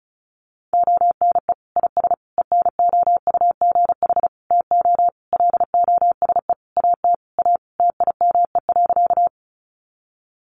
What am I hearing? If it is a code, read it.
Morse code, heard as a keyed tone, about 34 wpm, 700 Hz. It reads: ONE IS ENOUGH TO LOSE AT A TIME.
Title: Sentences from Top 1100 Words 35wpm